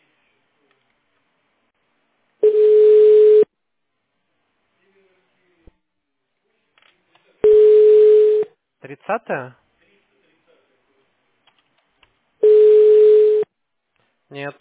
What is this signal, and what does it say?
Ringback tone (Europe, 425 Hz), with speech.
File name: 1010. .168dec57-c240-43d3-98d7